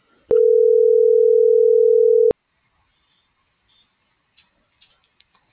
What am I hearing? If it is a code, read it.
Ringback tone (North America), 440 + 480 Hz.